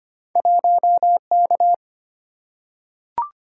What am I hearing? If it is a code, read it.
Morse code, heard as a keyed tone, about 25 wpm, 700 Hz. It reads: 1K
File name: Two Letter-Number Combinations - 25wpm